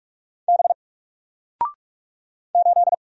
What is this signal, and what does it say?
Morse code, heard as a keyed tone, about 45 wpm, 700 Hz. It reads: B 8
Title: Single Letter-Number Alt 45wpm